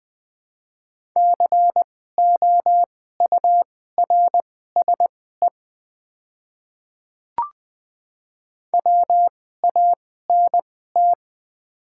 Morse code, heard as a keyed tone, about 20 wpm, spaced as for 14 wpm, 700 Hz. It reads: COURSE WANT
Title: 2x Character Spacing - Top 500 Words - 20wpm